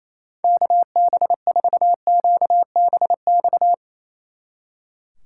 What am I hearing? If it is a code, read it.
Morse code, heard as a keyed tone, about 37 wpm, 700 Hz. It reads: KB4QBX